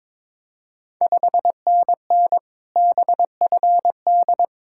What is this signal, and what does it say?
Morse code, heard as a keyed tone, about 22 wpm, 700 Hz. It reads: 5NN BFD
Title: Ontario QSO Party Exchange 22wpm